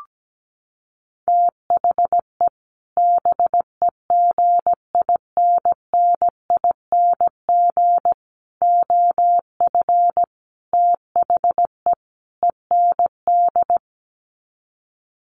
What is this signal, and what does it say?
Morse code, heard as a keyed tone, about 17 wpm, 700 Hz. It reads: THE BEGINNING OF THE END
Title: Sets of 5 Words Encore2 17wpm